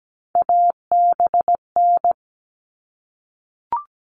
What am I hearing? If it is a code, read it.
Morse code, heard as a keyed tone, about 17 wpm, 700 Hz. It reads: ABN
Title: All Cut Numbers - 3-Digit 17wpm